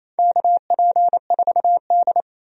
Morse code, heard as a keyed tone, about 28 wpm, 700 Hz. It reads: KP4D
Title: DXCC-100 Call Signs - 4 Characters - 28wpm